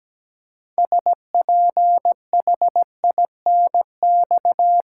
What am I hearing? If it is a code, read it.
Morse code, heard as a keyed tone, about 17 wpm, 700 Hz. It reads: SPHINX
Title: Pangram Challenge v2 - 17wpm